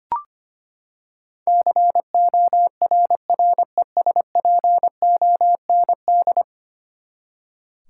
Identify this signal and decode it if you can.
Morse code, heard as a keyed tone, about 25 wpm, 700 Hz. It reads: CORRESPOND
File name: Ten Letter Words - 25wpm